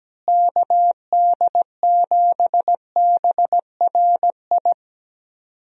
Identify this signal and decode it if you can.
Morse code, heard as a keyed tone, about 17 wpm, 700 Hz. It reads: KD7BRI
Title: USA Call Signs - 6 Characters - Repeated 3x - 17wpm